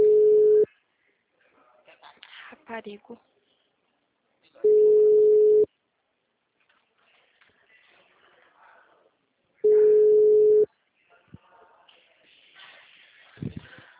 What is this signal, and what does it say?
Ringback tone (Europe, 425 Hz), with speech.